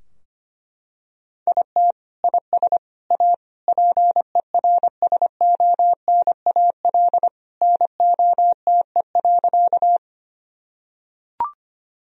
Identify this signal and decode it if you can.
Morse code, heard as a keyed tone, about 25 wpm, 700 Hz. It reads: IT IS A PERSONAL NOTE.
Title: Sentences from Top 1000 Words - Review Part 4 - 25wpm